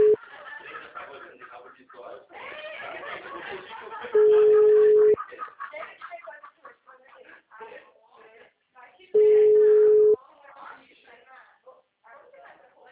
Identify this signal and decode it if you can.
Ringback tone (Europe, 425 Hz), with speech.
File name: out- -2025- - - .56597